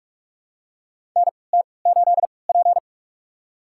Morse code, heard as a keyed tone, about 46 wpm, spaced as for 26 wpm, 700 Hz. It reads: NT8P
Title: USA Call Signs - 4 Characters - 3x Character Spacing - 45wpm